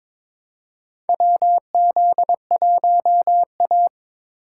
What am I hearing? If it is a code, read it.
Morse code, heard as a keyed tone, about 22 wpm, 700 Hz. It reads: WZ1A
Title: USA Call Signs - 4 Characters - 4x Character Spacing - 22wpm